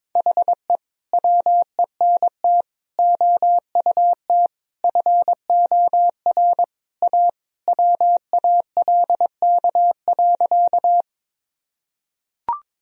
Morse code, heard as a keyed tone, about 22 wpm, 700 Hz. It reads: HE WENT OUT FOR A WALK.
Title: Sentences from Top 1000 Words - Review Part 1 - 22wpm